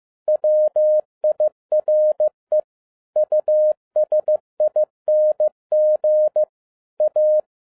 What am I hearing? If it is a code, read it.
Morse code, heard as a keyed tone, about 15 wpm, 610 Hz. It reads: WIRE USING A
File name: 2011-10-mp3 / n12-JL3YMV 15wpm